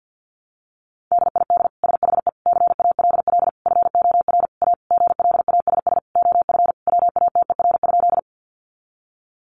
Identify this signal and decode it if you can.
Morse code, heard as a keyed tone, about 50 wpm, 700 Hz. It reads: DID SHE CALL FOR A GLASS OF WATER?
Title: Sentences from Top 1300 Words 50wpm